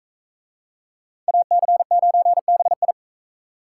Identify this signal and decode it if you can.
Morse code, heard as a keyed tone, about 44 wpm, 700 Hz. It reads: AC9BI